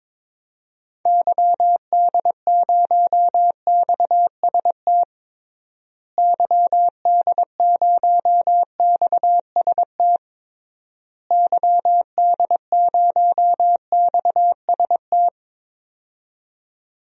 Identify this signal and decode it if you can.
Morse code, heard as a keyed tone, about 22 wpm, 700 Hz. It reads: YD0XST YD0XST YD0XST
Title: DXCC-100 Call Signs - 6 Characters - Repeated 3x - 22wpm